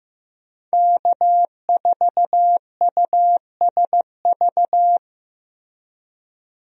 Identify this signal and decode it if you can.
Morse code, heard as a keyed tone, about 15 wpm, 700 Hz. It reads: K4USV
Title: USA Call Signs - 5 Characters - Repeated 2x - 15wpm